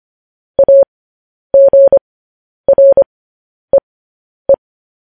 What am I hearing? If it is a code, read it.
Morse code, heard as a keyed tone, about 25 wpm, spaced as for 10 wpm, 550 Hz. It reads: AGREE
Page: news-headlines-in-morse-code-at-10-wpm